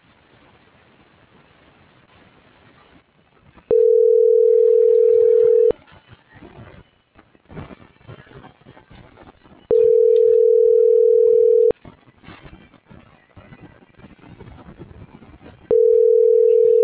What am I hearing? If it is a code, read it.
Ringback tone (North America), 440 + 480 Hz.